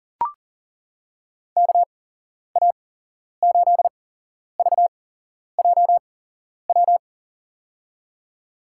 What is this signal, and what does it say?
Morse code, heard as a keyed tone, about 40 wpm, spaced as for 11 wpm, 700 Hz. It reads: KA8VJW
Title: USA Call Signs - 6 Characters - 8x Character Spacing - 40wpm